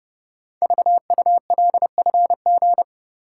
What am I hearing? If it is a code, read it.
Morse code, heard as a keyed tone, about 30 wpm, 700 Hz. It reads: VULFG
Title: Code Groups - 5 Letters - 30wpm